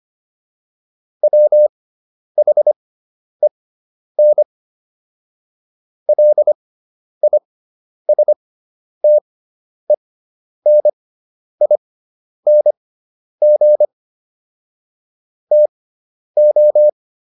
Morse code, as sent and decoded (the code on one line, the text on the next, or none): .-- .... -. / .-.. .. ... - . -. .. -. --. / - ---
WHEN LISTENING TO